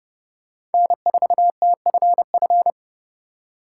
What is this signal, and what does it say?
Morse code, heard as a keyed tone, about 30 wpm, 700 Hz. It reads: N4TFF
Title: USA Call Signs - 5 Characters - 3x Character Spacing - 30wpm